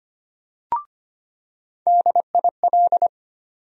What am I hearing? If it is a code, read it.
Morse code, heard as a keyed tone, about 25 wpm, 700 Hz. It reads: DIL
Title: USA Call Sign Postfixes - 3 Letters 25wpm